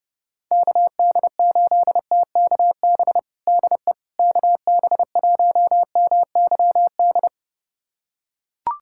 Morse code, heard as a keyed tone, about 30 wpm, 700 Hz. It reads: KD8TKB DE KB1MYD
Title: USA Callsign DE Callsign - 6 Characters - 30wpm